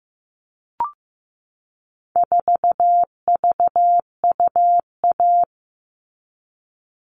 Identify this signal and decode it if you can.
Morse code, heard as a keyed tone, about 15 wpm, 700 Hz. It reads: 4VUA
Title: All Cut Numbers - 4-Digit 15wpm